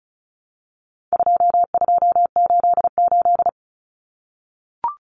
Morse code, heard as a keyed tone, about 34 wpm, 700 Hz. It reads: 2288
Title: Common Cut Numbers - 4-Digit 35wpm